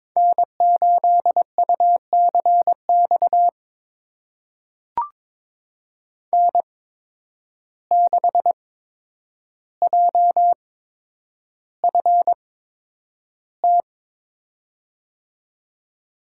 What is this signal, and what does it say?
Morse code, heard as a keyed tone, about 22 wpm, spaced as for 9 wpm, 700 Hz. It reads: N8UCX N6JFT
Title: USA Call Signs - 5 Characters - 8x Character Spacing - 22wpm